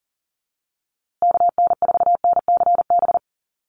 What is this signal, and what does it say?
Morse code, heard as a keyed tone, about 40 wpm, 700 Hz. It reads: KN4NCB
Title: USA Call Signs Repeated 3x 40wpm